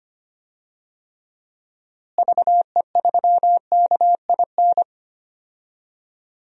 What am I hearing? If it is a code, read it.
Morse code, heard as a keyed tone, about 25 wpm, 700 Hz. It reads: VE3KIN
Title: DXCC-100 Call Signs - 6 Characters - Repeated 3x - 25wpm